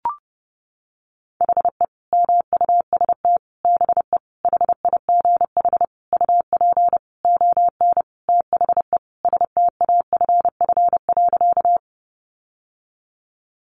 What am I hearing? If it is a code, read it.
Morse code, heard as a keyed tone, about 30 wpm, 700 Hz. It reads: HE MUST BE HIGH UP ON THE STAFF.
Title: Sentences from Top 1200 Words 30wpm